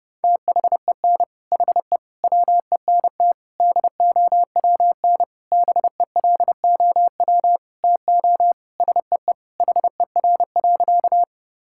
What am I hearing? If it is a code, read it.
Morse code, heard as a keyed tone, about 32 wpm, 700 Hz. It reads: THEN HE WENT DOWN BELOW TO SEE HER.
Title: Sentences from Top 1000 Words - Review Part 5 - 30wpm